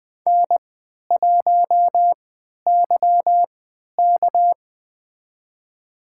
Morse code, heard as a keyed tone, about 20 wpm, spaced as for 11 wpm, 700 Hz. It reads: N1YK
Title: USA Call Signs - 4 Characters - 3x Character Spacing - 20wpm